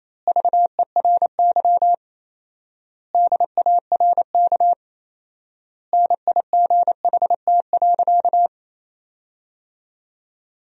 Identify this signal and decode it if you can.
Morse code, heard as a keyed tone, about 28 wpm, 700 Hz. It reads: VERY DARK NIGHT.
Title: 4x Word-Spacing - Sentences from Top 500 Words - 28wpm